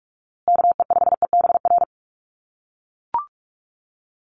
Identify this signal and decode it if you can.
Morse code, heard as a keyed tone, about 44 wpm, 700 Hz. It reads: KE5EBR
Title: USA Call Signs Repeated 2x 45wpm